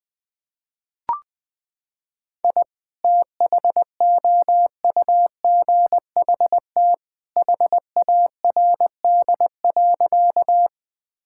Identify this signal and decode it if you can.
Morse code, heard as a keyed tone, about 20 wpm, 700 Hz. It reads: I THOUGHT HARD.
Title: Sentences with 2-Word Phrases - 20wpm